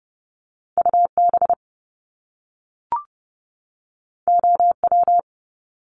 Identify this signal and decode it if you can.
Morse code, heard as a keyed tone, about 30 wpm, 700 Hz. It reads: UB OW
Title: Top Two Letter Combinations in Words 30wpm